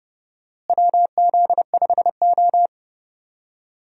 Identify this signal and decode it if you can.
Morse code, heard as a keyed tone, about 32 wpm, 700 Hz. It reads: WZ5O
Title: USA Call Signs - 4 Characters - Repeated 3x - 30wpm